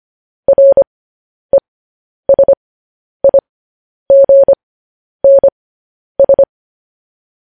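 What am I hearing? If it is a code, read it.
Morse code, heard as a keyed tone, about 25 wpm, spaced as for 10 wpm, 550 Hz. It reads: RESIGNS
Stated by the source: Morse code practice sending news headlines at 10 wpm.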